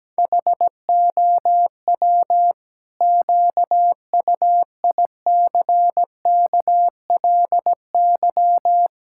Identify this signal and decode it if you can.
Morse code, heard as a keyed tone, about 17 wpm, 700 Hz. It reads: HOW QUICKLY
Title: Pangram Challenge v2 - 17wpm